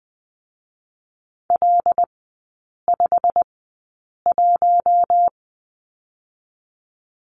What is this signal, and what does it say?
Morse code, heard as a keyed tone, about 21 wpm, spaced as for 8 wpm, 700 Hz. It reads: L51
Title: 087 Lesson 15e - Number 5 - Three Characters 20wpm